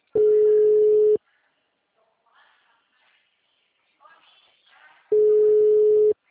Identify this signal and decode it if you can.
Ringback tone (Europe, 425 Hz), with speech.